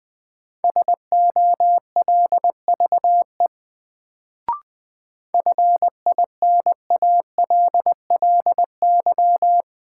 Morse code, heard as a keyed tone, about 20 wpm, 700 Hz. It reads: SOLVE FINALLY
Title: Sentences from Top 1000 Words - New Word Drill - 20wpm